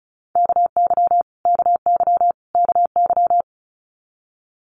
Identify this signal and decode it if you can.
Morse code, heard as a keyed tone, about 34 wpm, 700 Hz. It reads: KY KY KY
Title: US States Repeated 3x 35wpm